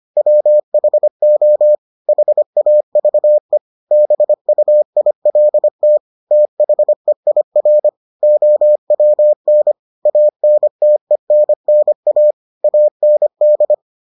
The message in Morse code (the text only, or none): WHO HAVE BUILT THEIR OWN ANTENNA AND